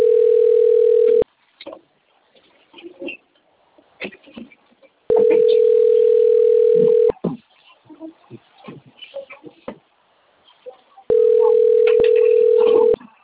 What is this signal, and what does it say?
Ringback tone (North America), 440 + 480 Hz.
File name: out- - - - - .86743